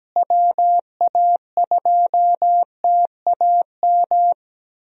Morse code, heard as a keyed tone, about 17 wpm, 700 Hz. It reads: WA2TAM
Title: USA Call Signs - 6 Characters - Repeated 3x - 17wpm